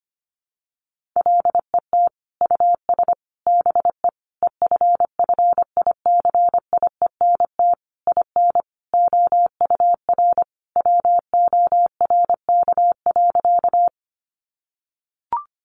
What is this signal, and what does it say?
Morse code, heard as a keyed tone, about 25 wpm, 700 Hz. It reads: LET US BE EFFICIENT IN OUR WORK.
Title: Sentences from Top 1300 Words 25wpm